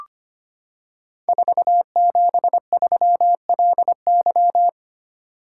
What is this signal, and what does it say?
Morse code, heard as a keyed tone, about 25 wpm, 700 Hz. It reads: 473LY
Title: Code Groups - 5 Letter:Numbers - 25wpm